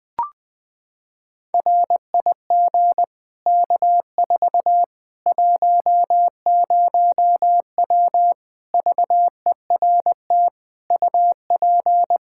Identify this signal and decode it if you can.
Morse code, heard as a keyed tone, about 21 wpm, 700 Hz. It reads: RIG K4 10W VERT UP
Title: Practice QSO - Exchange 2 - Info - 20wpm